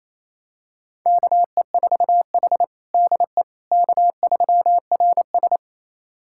Morse code, heard as a keyed tone, about 28 wpm, 690 Hz. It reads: KE4H DE K3RS
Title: USA Callsign DE Callsign - 4 Characters - 28wpm